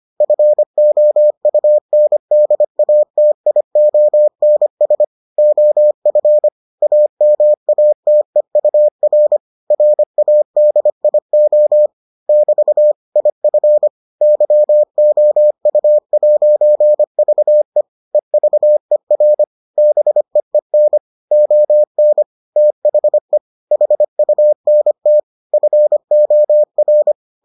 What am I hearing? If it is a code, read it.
Morse code, heard as a keyed tone, about 25 wpm, 600 Hz. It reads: FOUNDATIONS OF AMATEUR RADIO <BT> IF YOU'VE EVER BEEN ON THE HUNT FOR